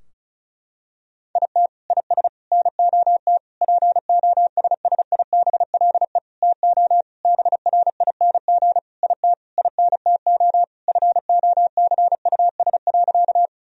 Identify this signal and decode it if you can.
Morse code, heard as a keyed tone, about 34 wpm, 700 Hz. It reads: IT IS NOT POSSIBLE TO BRING IT INTO FOCUS.
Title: Sentences from Top 1000 Words - Review Part 4 - 35wpm